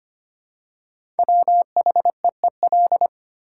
Morse code, heard as a keyed tone, about 25 wpm, 700 Hz. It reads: WHEEL
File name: 5 Letter Words - First 3 Letters Given - 25wpm